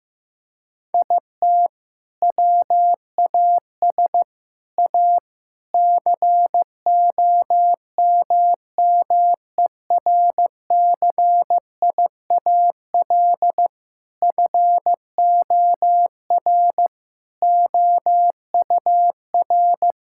Morse code, heard as a keyed tone, about 15 wpm, 700 Hz. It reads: IT WAS A COMMERCIAL FOR OUR
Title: Sentences from Top 1000 Words - Review Part 5 - 15wpm